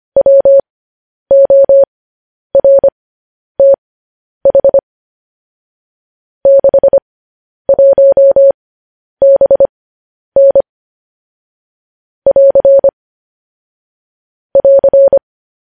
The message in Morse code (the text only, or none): WORTH 61BN <AR> <AR>